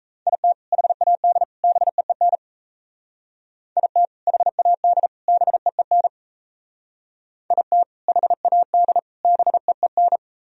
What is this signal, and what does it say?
Morse code, heard as a keyed tone, about 33 wpm, 700 Hz. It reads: IT HAD BEEN IT HAD BEEN IT HAD BEEN
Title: Speed-Racing - Sets of 3 Words 28wpm